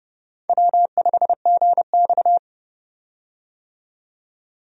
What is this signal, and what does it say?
Morse code, heard as a keyed tone, about 29 wpm, 700 Hz. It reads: W5GX
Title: USA Call Signs - 4 Characters - 30wpm